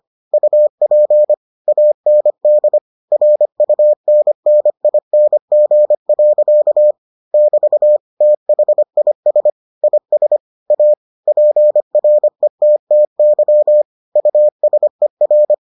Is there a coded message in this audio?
Yes: Morse code from 0.3 to 15.5 s, heard as a keyed tone, about 25 wpm, 600 Hz. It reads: UP AND RUNNING. <BT> THIS IS A PRETTY USER